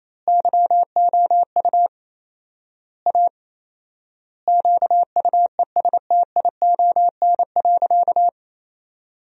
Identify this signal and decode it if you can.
Morse code, heard as a keyed tone, about 28 wpm, 700 Hz. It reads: YOU A QUESTION.
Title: 4x Word-Spacing - Sentences from Top 400 Words - 28wpm